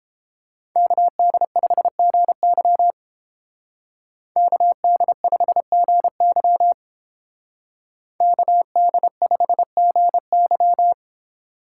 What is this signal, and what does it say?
Morse code, heard as a keyed tone, about 29 wpm, 700 Hz. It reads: KD5GY KD5GY KD5GY